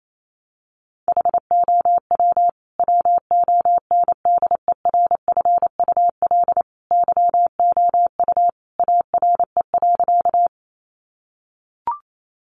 Morse code, heard as a keyed tone, about 28 wpm, 700 Hz. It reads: HOW WONDERFUL YOU ARE.
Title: Sentences from Top 1100 Words 28wpm